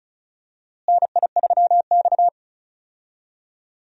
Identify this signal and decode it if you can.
Morse code, heard as a keyed tone, about 36 wpm, 700 Hz. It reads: NI3X
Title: USA Call Signs - 4 Characters - Repeated 2x - 35wpm